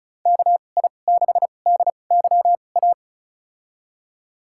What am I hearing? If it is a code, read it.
Morse code, heard as a keyed tone, about 36 wpm, spaced as for 25 wpm, 690 Hz. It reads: KI6DYA
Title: USA Call Signs - 6 Characters - 2x Character Spacing - 35wpm